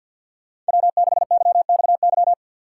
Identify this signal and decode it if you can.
Morse code, heard as a keyed tone, about 47 wpm, 690 Hz. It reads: W6YXY
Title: USA Call Signs - 5 Characters - 6x Character Spacing - 50wpm